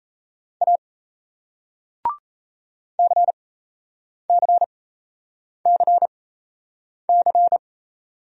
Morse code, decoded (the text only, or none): A CCCC